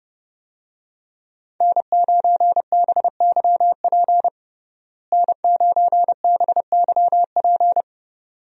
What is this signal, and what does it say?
Morse code, heard as a keyed tone, about 30 wpm, 700 Hz. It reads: N9BYP N9BYP
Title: USA Call Signs - 5 Characters - Repeated 2x - 30wpm